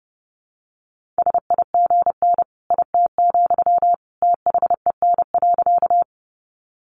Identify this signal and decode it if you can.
Morse code, heard as a keyed tone, about 30 wpm, 700 Hz. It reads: SIGN IT, THEN.
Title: Sentences from Top 1000 Words 30wpm